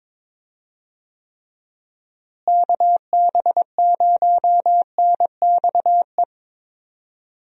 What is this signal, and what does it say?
Morse code, heard as a keyed tone, about 21 wpm, 700 Hz. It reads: KB0NXE
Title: USA Call Signs - 6 Characters - Repeated 3x - 22wpm